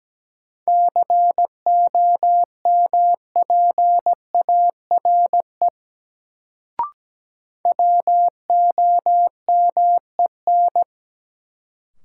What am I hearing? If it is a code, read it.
Morse code, heard as a keyed tone, about 17 wpm, 700 Hz. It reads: COMPARE WOMEN
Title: Sentences from Top 1300 Words - New Word Drill - 17wpm